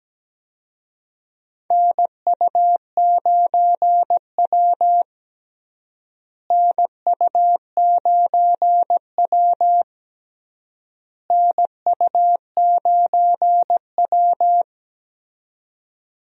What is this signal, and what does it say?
Morse code, heard as a keyed tone, about 17 wpm, 700 Hz. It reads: NU9W NU9W NU9W
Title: USA Call Signs - 4 Characters - Repeated 3x - 17wpm